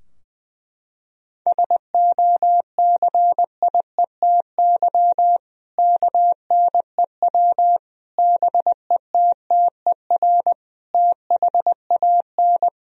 Morse code, heard as a keyed tone, about 20 wpm, 700 Hz. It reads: SOCIETY KNEW BETTER THAN
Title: Sentences from Top 1000 Words - Review Part 4 - 20wpm